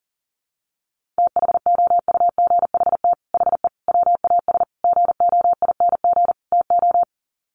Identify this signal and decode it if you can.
Morse code, heard as a keyed tone, about 40 wpm, 700 Hz. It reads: THOUGHT HE WAS GOING TO
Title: Sets of 5 Words Encore2 40wpm